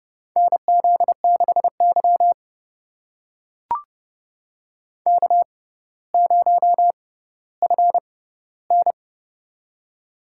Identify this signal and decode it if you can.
Morse code, heard as a keyed tone, about 30 wpm, spaced as for 15 wpm, 700 Hz. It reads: NZ6Y K0FN
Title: USA Call Signs - 6x Character Spacing - 30wpm